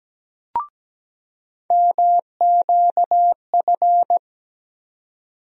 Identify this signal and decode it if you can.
Morse code, heard as a keyed tone, about 17 wpm, 700 Hz. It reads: MQF
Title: USA Call Sign Postfixes - 3 Letters 17wpm